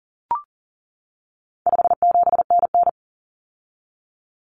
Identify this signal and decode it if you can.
Morse code, heard as a keyed tone, about 40 wpm, 700 Hz. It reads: 57NN
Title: Common Cut Numbers - 4-Digit 40wpm